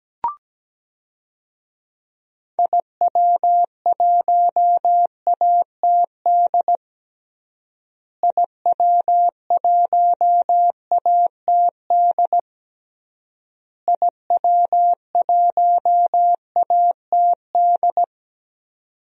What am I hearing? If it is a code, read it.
Morse code, heard as a keyed tone, about 17 wpm, 700 Hz. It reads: IW1ATD IW1ATD IW1ATD
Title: DXCC-100 Call Signs - 6 Characters - Repeated 3x - 17wpm